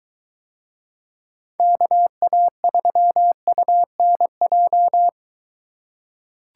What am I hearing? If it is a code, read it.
Morse code, heard as a keyed tone, about 21 wpm, 700 Hz. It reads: KA3UNJ